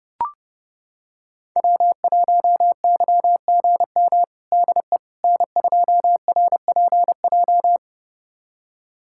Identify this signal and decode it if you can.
Morse code, heard as a keyed tone, about 29 wpm, 700 Hz. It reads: W1YGM DE N2RPJ
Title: USA Callsign DE Callsign - 5 Characters - 30wpm